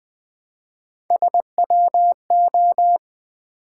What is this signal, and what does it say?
Morse code, heard as a keyed tone, about 20 wpm, 700 Hz. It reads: SWO
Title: Build A Word - 1 Letter At A Time - 3 to 5-Letter Words - 20wpm